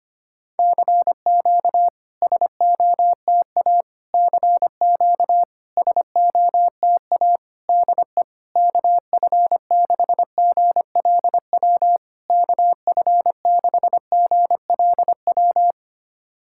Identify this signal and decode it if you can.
Morse code, heard as a keyed tone, about 25 wpm, 700 Hz. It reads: CQ SOTA CQ SOTA DE KF6GLW KF6GLW
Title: SOTA QSO Practice 25wpm